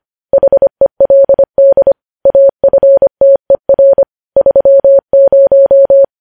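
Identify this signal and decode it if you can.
Morse code, heard as a keyed tone, about 25 wpm, 550 Hz. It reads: HELD AFTER 30
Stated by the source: News headlines sent in Morse code at 25 wpm.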